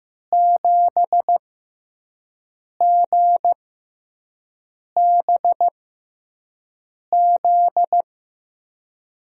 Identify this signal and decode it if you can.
Morse code, heard as a keyed tone, about 15 wpm, spaced as for 5 wpm, 700 Hz. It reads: 7GBZ
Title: USA Call Signs - 6x Character Spacing - 15wpm